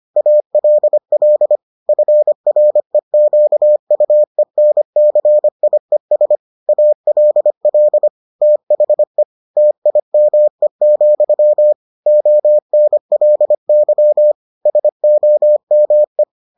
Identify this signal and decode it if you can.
Morse code, heard as a keyed tone, about 25 wpm, 600 Hz. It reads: ALL FREQUENCIES ALL THE TIME, ONLY SOME